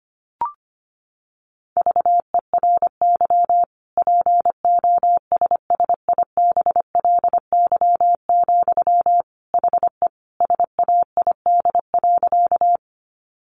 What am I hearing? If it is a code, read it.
Morse code, heard as a keyed tone, about 25 wpm, 700 Hz. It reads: VERY POSSIBLY, HE SAID.
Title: Sentences from Top 1300 Words 25wpm